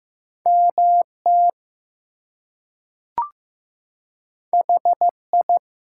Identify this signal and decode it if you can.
Morse code, heard as a keyed tone, about 15 wpm, 700 Hz. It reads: MT HI
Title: US States and Canadian Provinces 15wpm